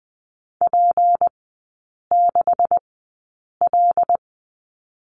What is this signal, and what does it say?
Morse code, heard as a keyed tone, about 20 wpm, spaced as for 8 wpm, 700 Hz. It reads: P6L
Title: 152 Lesson 25e - Letter p - Three Characters 20wpm